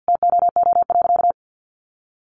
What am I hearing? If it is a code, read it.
Morse code, heard as a keyed tone, about 50 wpm, 700 Hz. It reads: TOO.
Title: Additional Easy Phrases 50wpm